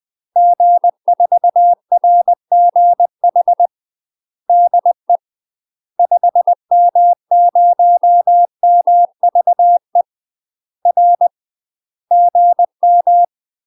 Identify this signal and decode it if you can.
Morse code, heard as a keyed tone, about 20 wpm, 700 Hz. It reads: G4RGH DE 5M0MVE R GM
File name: UK MASTER QSOs TRAINING / 20WPM / UK MASTER QSOs 0032